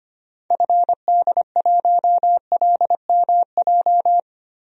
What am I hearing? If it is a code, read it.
Morse code, heard as a keyed tone, about 25 wpm, 700 Hz. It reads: FD1LMJ
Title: DXCC-100 Call Signs - 6 Characters - Repeated 2x - 25wpm